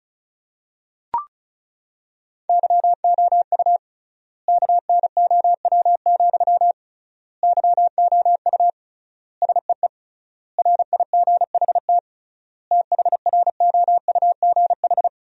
Morse code, heard as a keyed tone, about 36 wpm, 700 Hz. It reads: YOU KNOW, YOU SEE RIGHT THROUGH
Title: 3x Word-Spacing - Sentences from Top 200 Words - 35wpm